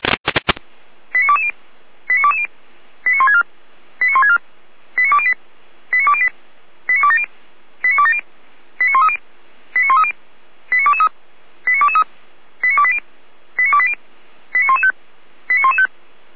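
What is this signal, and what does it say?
Five-tone selective call (ZVEI), heard as quick runs of five tones, tones of about 80 ms. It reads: 89220 89220 89164 89164 89208 89208 89270 89270 89130 89130 89223 89223 89280 89280 89115 89115
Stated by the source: ZVEI five-tone selective call 89192 received with an SDR.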